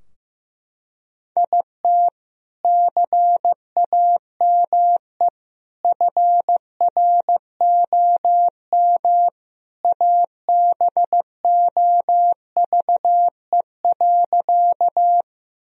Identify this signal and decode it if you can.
Morse code, heard as a keyed tone, about 15 wpm, 700 Hz. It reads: IT CAME FROM ABOVE.
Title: Sentences from Top 1000 Words - Review Part 2 - 15wpm